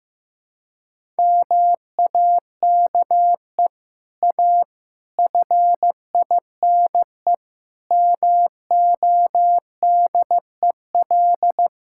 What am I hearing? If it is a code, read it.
Morse code, heard as a keyed tone, about 15 wpm, 700 Hz. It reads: MAKE A FINE MODEL
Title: Sentences from Top 1000 Words - Review Part 5 - 15wpm